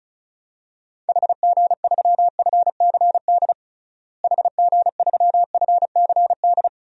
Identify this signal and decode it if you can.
Morse code, heard as a keyed tone, about 34 wpm, 700 Hz. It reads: HG3FCD HG3FCD
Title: DXCC-100 Call Signs - 6 Characters - Repeated 2x - 35wpm